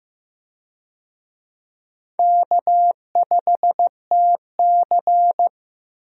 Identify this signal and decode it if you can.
Morse code, heard as a keyed tone, about 15 wpm, 700 Hz. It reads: K5TC
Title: USA Call Signs - 4 Characters - Repeated 2x - 15wpm